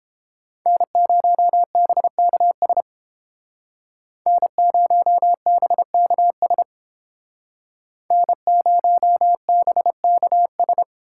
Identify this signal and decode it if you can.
Morse code, heard as a keyed tone, about 32 wpm, 700 Hz. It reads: N0BKS N0BKS N0BKS